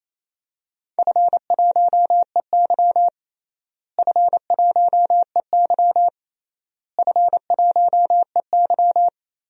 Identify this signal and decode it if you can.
Morse code, heard as a keyed tone, about 28 wpm, 700 Hz. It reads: F1EY F1EY F1EY
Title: DXCC-100 Call Signs - 4 Characters - Repeated 3x - 28wpm